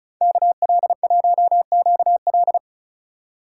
Morse code, heard as a keyed tone, about 44 wpm, 700 Hz. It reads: KL1QL